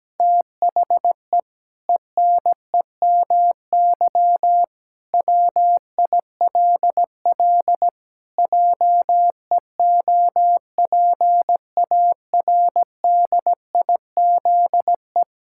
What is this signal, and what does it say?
Morse code, heard as a keyed tone, about 17 wpm, 700 Hz. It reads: THE ENEMY WILL JEOPARDIZE
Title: Pangram Challenge v2 - 17wpm